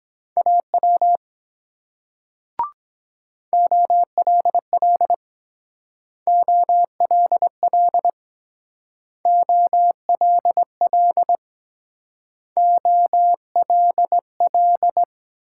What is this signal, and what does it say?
Morse code, heard as a keyed tone, about 22 wpm, 700 Hz. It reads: AW OLL OLL OLL OLL